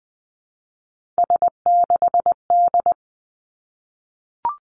Morse code, heard as a keyed tone, about 20 wpm, 700 Hz. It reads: S6D
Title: Three Challenging Letter-Number Combinations 20wpm